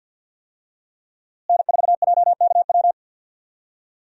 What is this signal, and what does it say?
Morse code, heard as a keyed tone, about 48 wpm, 700 Hz. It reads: N4JKW
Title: USA Call Signs - 5 Characters - Repeated 3x - 50wpm